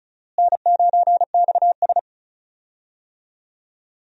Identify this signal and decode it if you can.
Morse code, heard as a keyed tone, about 34 wpm, 700 Hz. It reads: N9XS
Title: USA Call Signs - 4 Characters - Repeated 3x - 35wpm